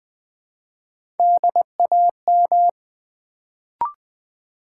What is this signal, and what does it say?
Morse code, heard as a keyed tone, about 20 wpm, 700 Hz. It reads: DAM
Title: Build A Word - 1 Letter At A Time - 3-Letter Words - 20wpm